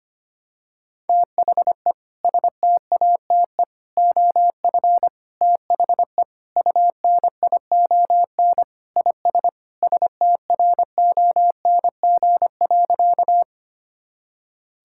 Morse code, heard as a keyed tone, about 25 wpm, 700 Hz. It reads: THE STATE OF THE UNION IS STRONG.
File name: practice / 25wpm / Sentences with 3-Word Phrases - 25wpm